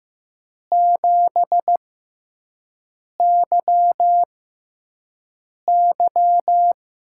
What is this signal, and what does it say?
Morse code, heard as a keyed tone, about 15 wpm, spaced as for 5 wpm, 700 Hz. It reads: 7YY
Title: USA Call Signs - 5 Characters - 6x Character Spacing - 15wpm